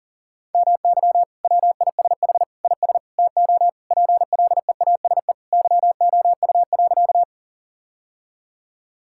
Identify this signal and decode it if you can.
Morse code, heard as a keyed tone, about 40 wpm, 700 Hz. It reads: MY WISH IS TO PLEASE YOU.
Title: Sentences from Top 1600 Words 40wpm